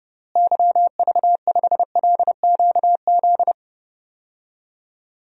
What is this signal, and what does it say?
Morse code, heard as a keyed tone, about 31 wpm, 700 Hz. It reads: YV5LQZ
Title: DXCC-100 Call Signs - 6 Characters - Repeated 2x - 30wpm